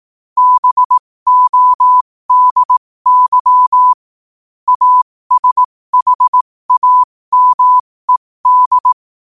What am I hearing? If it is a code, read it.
Morse code, heard as a keyed tone, about 18 wpm, spaced as for 15 wpm, 1000 Hz. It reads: BODY ASHAMED
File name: qotdfiles / 15wpm